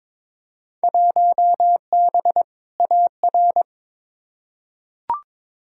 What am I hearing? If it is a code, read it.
Morse code, heard as a keyed tone, about 22 wpm, 700 Hz. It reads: 1B AR
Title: Field Day Exchanges Encore - 22wpm